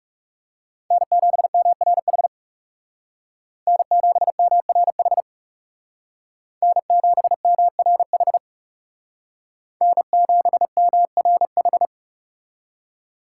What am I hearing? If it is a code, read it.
Morse code, heard as a keyed tone, about 36 wpm, 700 Hz. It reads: N7MRH N7MRH N7MRH N7MRH
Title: Speed Racing - USA 1x3 Call Signs 30wpm